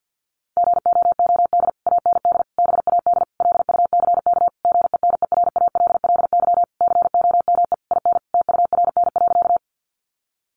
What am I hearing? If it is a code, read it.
Morse code, heard as a keyed tone, about 50 wpm, 700 Hz. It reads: GOOD AND BAD LUCK GENERALLY COME IN TURN.
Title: Sentences from Top 1500 Words 50wpm